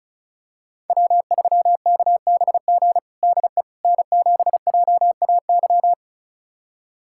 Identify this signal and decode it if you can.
Morse code, heard as a keyed tone, about 34 wpm, 700 Hz. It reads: W3KBG DE N7JAY
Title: USA Callsign DE Callsign - 5 Characters - 35wpm